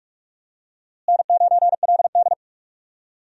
Morse code, heard as a keyed tone, about 46 wpm, 700 Hz. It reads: N9LD
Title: USA Call Signs - 4 Characters - Repeated 3x - 45wpm